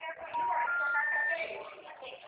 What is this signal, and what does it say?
Special information tone, with speech.
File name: out- -2009- - - .143